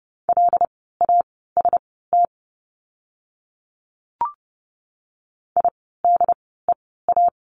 Morse code, heard as a keyed tone, about 30 wpm, spaced as for 17 wpm, 700 Hz. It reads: LAST IDEA
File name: course / 30:10wpm / 108 Lesson 18f - Letter m - Words 30wpm